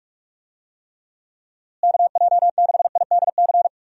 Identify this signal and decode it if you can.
Morse code, heard as a keyed tone, about 42 wpm, 700 Hz. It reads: KJ6IDC
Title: USA Call Signs - 6 Characters - Repeated 2x - 45wpm